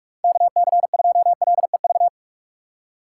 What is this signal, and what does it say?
Morse code, heard as a keyed tone, about 42 wpm, 700 Hz. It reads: KC2LEV
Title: USA Call Signs - 6 Characters - 4x Character Spacing - 45wpm